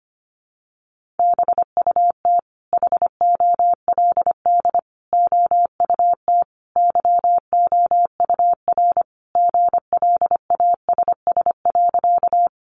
Morse code, heard as a keyed tone, about 25 wpm, 700 Hz. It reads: BUT HOLD OUT YOUR GLASS.
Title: Sentences from Top 1300 Words 25wpm